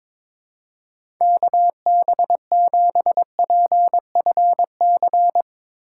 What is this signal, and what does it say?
Morse code, heard as a keyed tone, about 21 wpm, 700 Hz. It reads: KB7PFC
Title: USA Call Signs - 3x Character Spacing - 22wpm